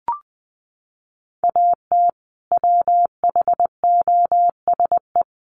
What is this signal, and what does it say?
Morse code, heard as a keyed tone, about 20 wpm, 700 Hz. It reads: AT WHOSE
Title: Sentences from Top 1200 Words 20wpm